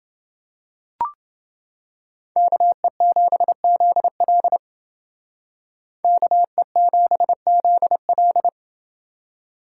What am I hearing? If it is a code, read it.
Morse code, heard as a keyed tone, about 29 wpm, 700 Hz. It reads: KE7ZL KE7ZL